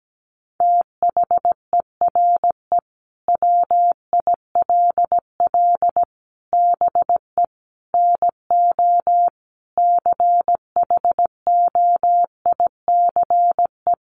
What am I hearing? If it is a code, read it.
Morse code, heard as a keyed tone, about 17 wpm, 700 Hz. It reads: THERE WILL BE NO CHOICE
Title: Sentences from Top 1500 Words 17wpm